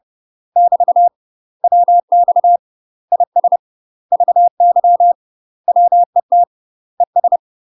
Morse code, heard as a keyed tone, about 30 wpm, 700 Hz. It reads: <BT> WX IS VY WET ES
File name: UK MASTER QSOs TRAINING / 30WPM / UK MASTER QSOs 0049